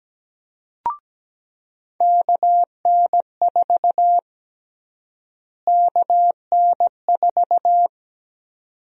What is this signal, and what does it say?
Morse code, heard as a keyed tone, about 17 wpm, 700 Hz. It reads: KN4 KN4
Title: USA Call Sign Prefixes - Repeated 2x - 17wpm